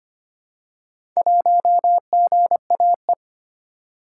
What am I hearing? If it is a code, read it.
Morse code, heard as a keyed tone, about 25 wpm, 700 Hz. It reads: 1GAE
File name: practice / 25wpm / Four Letter-Number Combinations - 25wpm